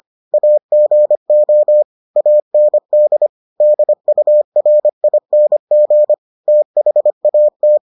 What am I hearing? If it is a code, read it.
Morse code, heard as a keyed tone, about 25 wpm, 600 Hz. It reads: AGO AND DURING THAT